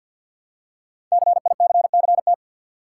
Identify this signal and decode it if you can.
Morse code, heard as a keyed tone, about 50 wpm, 700 Hz. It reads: CICCT
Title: Code Groups - 5 Letters - 50wpm